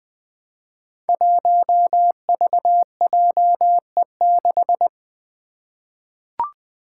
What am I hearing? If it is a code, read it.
Morse code, heard as a keyed tone, about 20 wpm, 700 Hz. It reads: 1VJE6
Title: Five Challenging Letter-Number Combinations 20wpm